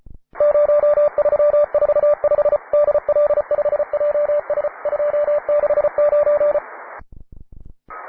Morse code, heard as a keyed tone, about 34 wpm, 590 Hz, with radio noise. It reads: 0345 DL5JS 269
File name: wae / cw advanced / py2na cw